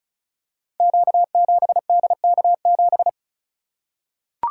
Morse code, heard as a keyed tone, about 36 wpm, 690 Hz. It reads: Q7DK7
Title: Code Groups - 5 Letter:Numbers - 35wpm